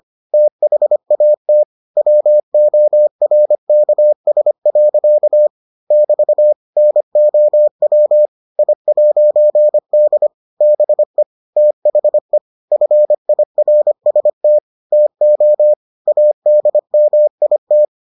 Morse code, heard as a keyed tone, about 25 wpm, 600 Hz. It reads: THAT WORKS. <BT> NOW I'D BE THE FIRST TO ADMIT